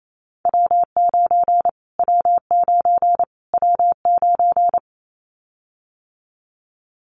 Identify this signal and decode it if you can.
Morse code, heard as a keyed tone, about 28 wpm, 700 Hz. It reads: W9 W9 W9
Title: Callsign Prefixes for Top 25 DXCC - Repeated 3x - 28wpm